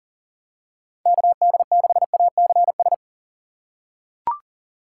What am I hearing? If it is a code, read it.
Morse code, heard as a keyed tone, about 40 wpm, 690 Hz. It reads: KD6ACS
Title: USA Call Signs - 6x Character Spacing - 40wpm